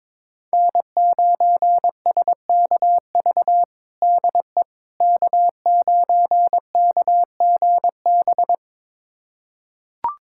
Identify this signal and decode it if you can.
Morse code, heard as a keyed tone, about 22 wpm, 700 Hz. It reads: N9SKV DE K9KGB
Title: USA Callsign DE Callsign - 5 Characters - 22wpm